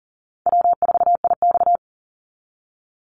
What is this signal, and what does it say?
Morse code, heard as a keyed tone, about 40 wpm, 700 Hz. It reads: W4IX
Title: Call Signs Repeated 2x - 40wpm